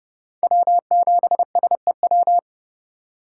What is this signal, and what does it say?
Morse code, heard as a keyed tone, about 32 wpm, 700 Hz. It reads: W7SEW
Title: USA Call Signs - 5 Characters - 2x Character Spacing - 30wpm